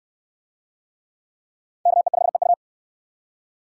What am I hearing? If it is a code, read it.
Morse code, heard as a keyed tone, about 68 wpm, 700 Hz. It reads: KE5EU